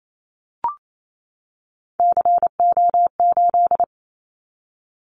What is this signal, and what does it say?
Morse code, heard as a keyed tone, about 28 wpm, 700 Hz. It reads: CO8
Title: Callsign Prefixes for Top 100 DXCC - 28wpm